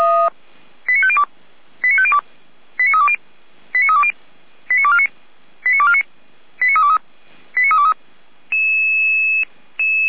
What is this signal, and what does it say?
Five-tone selective call (ZVEI), heard as quick runs of five tones, tones of about 70 ms. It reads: 89502 89502 89320 89320 89249 89249 89323 89323